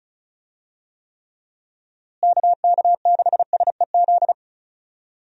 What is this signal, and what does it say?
Morse code, heard as a keyed tone, about 36 wpm, 700 Hz. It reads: KK6SEZ
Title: USA Call Signs - 6 Characters - Repeated 2x - 35wpm